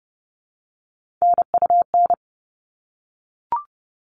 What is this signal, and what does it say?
Morse code, heard as a keyed tone, about 30 wpm, 700 Hz. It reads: NUN
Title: All Cut Numbers - 3-Digit 30wpm